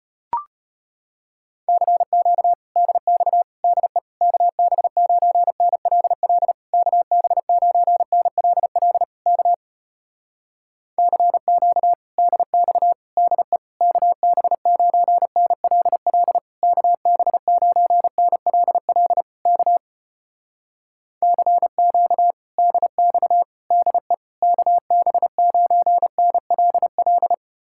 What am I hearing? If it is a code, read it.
Morse code, heard as a keyed tone, about 33 wpm, 700 Hz. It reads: CQ DX DE KB9NLL KB9NLL K CQ DX DE KB9NLL KB9NLL K CQ DX DE KB9NLL